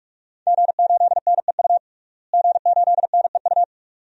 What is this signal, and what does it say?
Morse code, heard as a keyed tone, about 46 wpm, 700 Hz. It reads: G8NEU G8NEU
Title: DXCC-100 Call Signs - 5 Characters - Repeated 2x - 45wpm